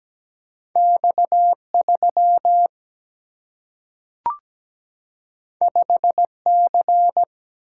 Morse code, heard as a keyed tone, about 17 wpm, 700 Hz. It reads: X3 5C
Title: Single-Letter Single-Number 17wpm